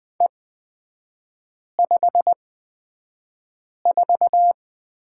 Morse code, heard as a keyed tone, about 20 wpm, spaced as for 5 wpm, 700 Hz. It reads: E54
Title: 224 Lesson 36e - BT - Three Characters 20wpm